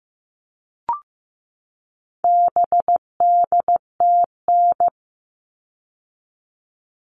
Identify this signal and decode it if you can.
Morse code, heard as a keyed tone, about 15 wpm, 700 Hz. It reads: BDTN
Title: All Cut Numbers - 4-Digit 15wpm